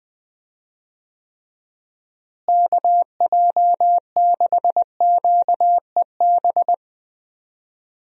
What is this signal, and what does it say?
Morse code, heard as a keyed tone, about 21 wpm, 700 Hz. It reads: KJ6QEB
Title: USA Call Signs - 6 Characters - Repeated 2x - 20wpm